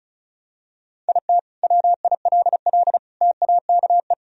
Morse code, heard as a keyed tone, about 36 wpm, 700 Hz. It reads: IT WILL TAKE
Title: Sentences from Top 1000 Words - Review Part 1 - 35wpm